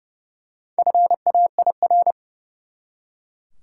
Morse code, heard as a keyed tone, about 30 wpm, 700 Hz. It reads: FAIR
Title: Four Letter Words - 30wpm